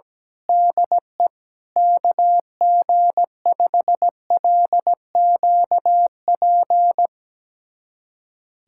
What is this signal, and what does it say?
Morse code, heard as a keyed tone, about 17 wpm, 700 Hz. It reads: DE KG5LQP
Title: USA Callsign DE Callsign - 6 Characters - 17wpm